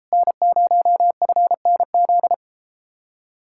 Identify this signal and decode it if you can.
Morse code, heard as a keyed tone, about 33 wpm, 700 Hz. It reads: N0FNZ